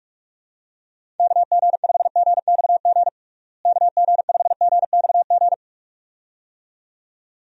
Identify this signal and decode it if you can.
Morse code, heard as a keyed tone, about 44 wpm, 700 Hz. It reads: KG5GXG KG5GXG
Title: USA Call Signs - 6 Characters - Repeated 2x - 45wpm